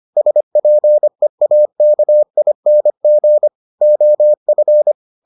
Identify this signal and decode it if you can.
Morse code, heard as a keyed tone, about 25 wpm, 600 Hz. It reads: SPEAKING OF